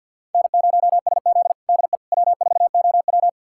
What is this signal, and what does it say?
Morse code, heard as a keyed tone, about 50 wpm, 690 Hz. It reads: N0SZ DE W4OW
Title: USA Callsign DE Callsign - 4 Characters - 50wpm